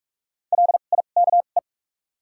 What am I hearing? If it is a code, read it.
Morse code, heard as a keyed tone, about 48 wpm, spaced as for 33 wpm, 700 Hz. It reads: LIKE
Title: 2x Character Spacing - Top 100 Words - 45wpm